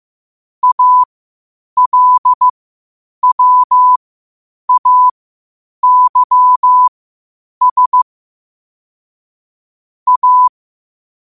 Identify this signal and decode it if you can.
Morse code, heard as a keyed tone, about 15 wpm, spaced as for 8 wpm, 1000 Hz. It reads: ALWAYS A
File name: qotdfiles / 8wpm